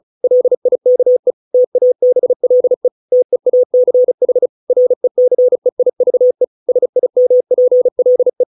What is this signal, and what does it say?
Morse code, heard as a keyed tone, about 34 wpm, 500 Hz. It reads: LIKE TABLE TEACH RECEIVE SIMPLE